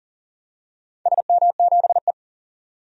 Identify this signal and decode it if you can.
Morse code, heard as a keyed tone, about 40 wpm, 700 Hz. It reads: SM7E
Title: DXCC-100 Call Signs - 4 Characters - 40wpm